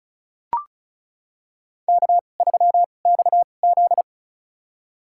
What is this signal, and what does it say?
Morse code, heard as a keyed tone, about 36 wpm, spaced as for 25 wpm, 700 Hz. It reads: K3XZ
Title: USA Call Signs - 4 Characters - 2x Character Spacing - 35wpm